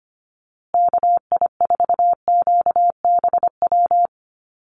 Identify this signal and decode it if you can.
Morse code, heard as a keyed tone, about 25 wpm, 700 Hz. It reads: KI4QBW
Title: USA Call Signs Repeated 2x 25wpm